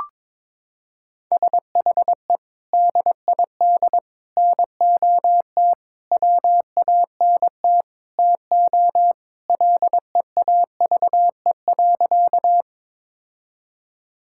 Morse code, heard as a keyed tone, about 22 wpm, 700 Hz. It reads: SHE DID NOT WANT TO LEAVE.
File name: practice / 22wpm / Sentences with 4-Word Phrases - 22wpm